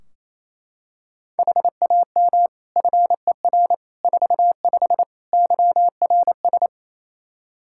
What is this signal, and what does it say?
Morse code, heard as a keyed tone, about 27 wpm, 700 Hz. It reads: HAM FER 45 YRS
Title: Practice QSO - Exchange 3 - More Info - 28wpm